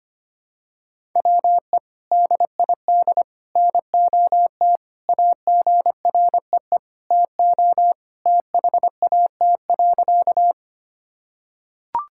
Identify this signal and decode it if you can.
Morse code, heard as a keyed tone, about 25 wpm, 700 Hz. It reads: WE DID NOT AGREE TO THAT.
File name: practice / 25wpm / Sentences with 3-Word Phrases - 25wpm